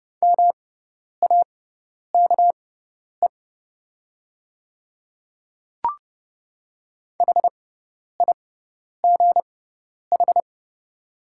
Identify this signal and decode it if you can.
Morse code, heard as a keyed tone, about 30 wpm, spaced as for 10 wpm, 690 Hz. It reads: MAKE HIGH
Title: 6x Character Spacing - Top 200 Words - 30wpm